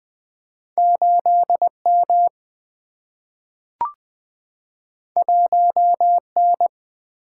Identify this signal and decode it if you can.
Morse code, heard as a keyed tone, about 20 wpm, 700 Hz. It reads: 8 M 1 N
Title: Single-Letter Single-Number 20wpm